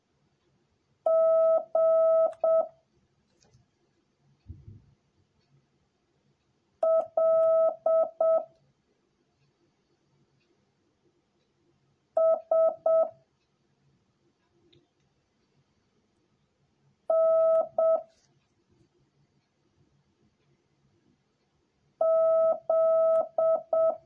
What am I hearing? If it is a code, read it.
Morse code, heard as a keyed tone, about 7 wpm, 650 Hz. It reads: G L S N Z